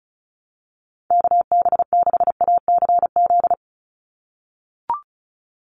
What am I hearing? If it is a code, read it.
Morse code, heard as a keyed tone, about 34 wpm, 700 Hz. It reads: KB6ACZ
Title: USA Call Signs 35wpm